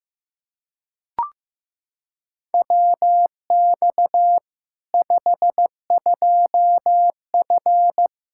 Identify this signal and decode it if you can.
Morse code, heard as a keyed tone, about 14 wpm, 700 Hz. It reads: WX 52F
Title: K1USN SST QSO Practice 15wpm